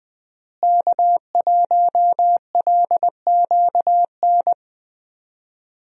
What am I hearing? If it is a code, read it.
Morse code, heard as a keyed tone, about 20 wpm, 700 Hz. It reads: K1LQN
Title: USA Call Signs - 5 Characters - Repeated 2x - 20wpm